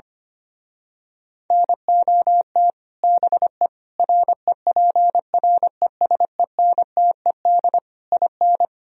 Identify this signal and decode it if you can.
Morse code, heard as a keyed tone, about 24 wpm, 700 Hz. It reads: NOT BE REPRESENTED IN
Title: Speed-Racing - Sets of 5 Words 25wpm